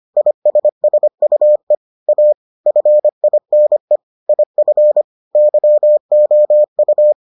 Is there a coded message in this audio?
Yes: Morse code from 0.2 to 7.1 s, heard as a keyed tone, about 25 wpm, 600 Hz. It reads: ISSUE A FINE IF YOU